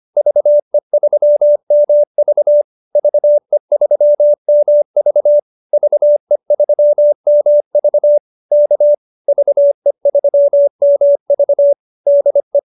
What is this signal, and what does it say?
Morse code, heard as a keyed tone, about 25 wpm, 590 Hz. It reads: VE3MV VE3MV VE3MV K VE3MV DE